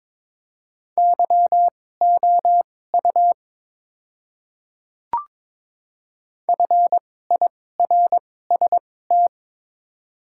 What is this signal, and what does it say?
Morse code, heard as a keyed tone, about 22 wpm, spaced as for 16 wpm, 700 Hz. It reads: YOU FIRST
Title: 2x Character Spacing - Top 100 Words - 22wpm